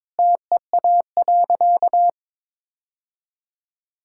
Morse code, heard as a keyed tone, about 22 wpm, 700 Hz. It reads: TEA.
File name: practice / 22wpm / Sentences with 2-Word Phrases - 22wpm